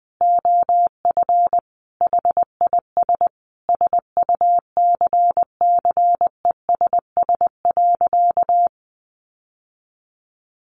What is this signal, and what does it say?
Morse code, heard as a keyed tone, about 20 wpm, 700 Hz. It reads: OF HIS SUCCESS.
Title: Sentences from Top 900 Words 20wpm